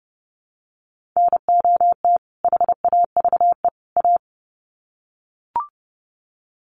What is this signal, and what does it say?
Morse code, heard as a keyed tone, about 30 wpm, 700 Hz. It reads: NOT HAVE A
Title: Sets of 3 Words 30wpm